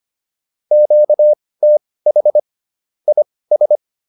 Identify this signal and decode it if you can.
Morse code, heard as a keyed tone, about 25 wpm, spaced as for 18 wpm, 600 Hz. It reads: QTH IS